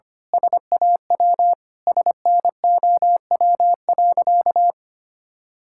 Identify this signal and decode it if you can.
Morse code, heard as a keyed tone, about 25 wpm, 700 Hz. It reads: SAW SNOW.
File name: practice / 25wpm / Sentences with 3-Word Phrases - 25wpm